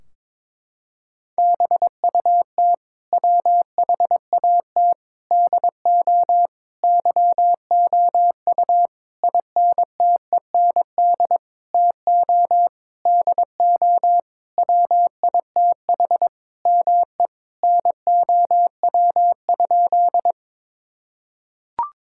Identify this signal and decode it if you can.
Morse code, heard as a keyed tone, about 22 wpm, 700 Hz. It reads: BUT WHAT DO YOU INTEND TO DO WITH ME NOW?
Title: Sentences from Top 1600 Words 22wpm